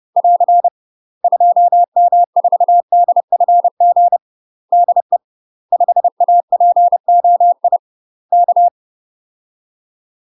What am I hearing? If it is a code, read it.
Morse code, heard as a keyed tone, about 30 wpm, 700 Hz. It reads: <AR> 2M4DFG DE 5APOI K
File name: UK MASTER QSOs TRAINING / 30WPM / UK MASTER QSOs 0033